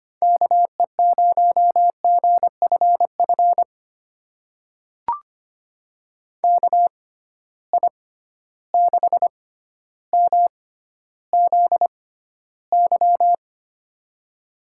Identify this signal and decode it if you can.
Morse code, heard as a keyed tone, about 25 wpm, spaced as for 13 wpm, 700 Hz. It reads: KE0GFF KI6MZY
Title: USA Call Signs - 6 Characters - 6x Character Spacing - 25wpm